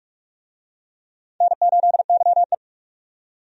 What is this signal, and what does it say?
Morse code, heard as a keyed tone, about 43 wpm, 700 Hz. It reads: N8YE
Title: USA Call Signs - 4 Characters - 45wpm